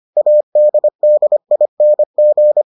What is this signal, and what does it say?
Morse code, heard as a keyed tone, about 25 wpm, 600 Hz. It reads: ADDING